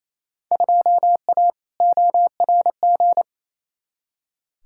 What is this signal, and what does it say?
Morse code, heard as a keyed tone, about 28 wpm, 700 Hz. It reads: 2A ORG
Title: Field Day Exchanges Encore - 28wpm